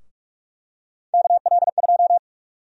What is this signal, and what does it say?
Morse code, heard as a keyed tone, about 46 wpm, 700 Hz. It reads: KL2
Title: USA Call Sign Prefixes - Repeated 2x - 45wpm